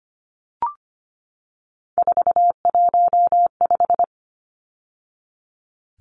Morse code, heard as a keyed tone, about 25 wpm, 700 Hz. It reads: 415
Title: Numbers - 3-Digit 25wpm